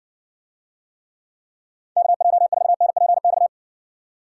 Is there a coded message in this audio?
Yes: Morse code from 2.0 to 3.5 s, heard as a keyed tone, about 60 wpm, 700 Hz. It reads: KJ4NPX